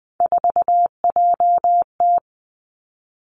Morse code, heard as a keyed tone, about 20 wpm, 700 Hz. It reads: 4JT
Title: Three Challenging Letter-Number Combinations 20wpm